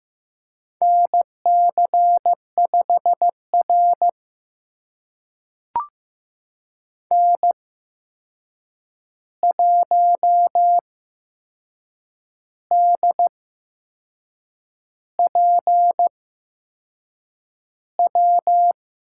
Morse code, heard as a keyed tone, about 15 wpm, spaced as for 6 wpm, 700 Hz. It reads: NC5R N1DPW
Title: USA Call Signs - 8x Character Spacing - 15wpm